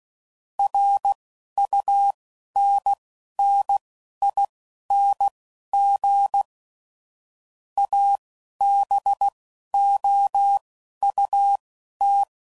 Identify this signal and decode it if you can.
Morse code, heard as a keyed tone, about 16 wpm, spaced as for 11 wpm, 800 Hz. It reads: RUNNING ABOUT
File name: CW Code Groups @15WPM / QSOs / WD0FBT Spaced